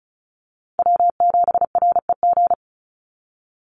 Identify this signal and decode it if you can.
Morse code, heard as a keyed tone, about 34 wpm, 700 Hz. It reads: W7REG
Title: USA Call Signs 35wpm